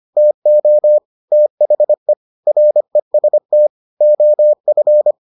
Morse code, heard as a keyed tone, about 25 wpm, 600 Hz. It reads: TO THE REST OF